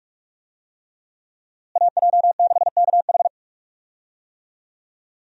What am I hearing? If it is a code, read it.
Morse code, heard as a keyed tone, about 46 wpm, 700 Hz. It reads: AJ6KH
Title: USA Call Signs - 5 Characters - Repeated 3x - 45wpm